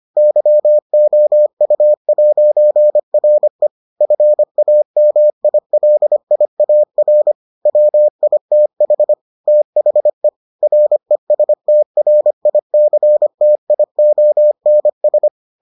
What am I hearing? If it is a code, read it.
Morse code, heard as a keyed tone, about 25 wpm, 600 Hz. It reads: YOU'RE FAMILIAR WITH THE RESTRICTIONS